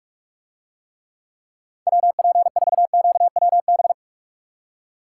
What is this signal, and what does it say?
Morse code, heard as a keyed tone, about 45 wpm, 700 Hz. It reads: WP4QWB